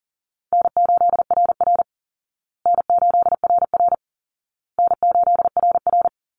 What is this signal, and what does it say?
Morse code, heard as a keyed tone, about 40 wpm, 700 Hz. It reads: N8RR N8RR N8RR
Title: Easy Call Signs Repeated 3x - 40wpm